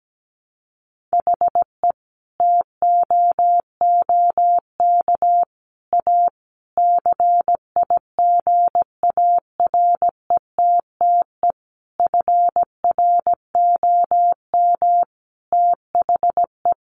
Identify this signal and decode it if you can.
Morse code, heard as a keyed tone, about 17 wpm, 700 Hz. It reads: HE TOOK A CIGARETTE FROM THE
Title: Sentences from Top 1000 Words 17wpm